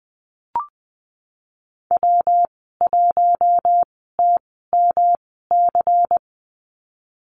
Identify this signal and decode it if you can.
Morse code, heard as a keyed tone, about 19 wpm, spaced as for 14 wpm, 700 Hz. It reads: W1TMC
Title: 167 Lesson 27g - Number 7 - Call Signs 20wpm